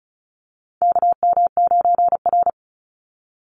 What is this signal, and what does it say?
Morse code, heard as a keyed tone, about 34 wpm, 700 Hz. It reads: KM9R
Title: Call Signs - 35wpm